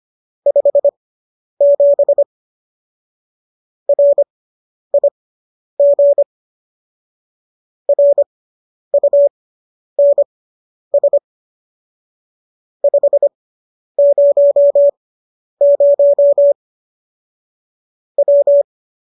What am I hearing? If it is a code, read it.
Morse code, heard as a keyed tone, about 25 wpm, spaced as for 10 wpm, 570 Hz. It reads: 57 RIG RUNS 500 W